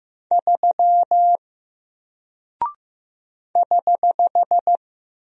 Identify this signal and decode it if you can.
Morse code, heard as a keyed tone, about 15 wpm, 700 Hz. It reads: 3 <HH>
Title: Single Character (Letter:Number:Punctuation:Prosign) 15wpm